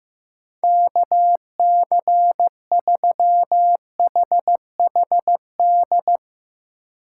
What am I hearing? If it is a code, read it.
Morse code, heard as a keyed tone, about 15 wpm, 700 Hz. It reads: KC3HHD
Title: USA Call Signs - 6 Characters - 3x Character Spacing - 15wpm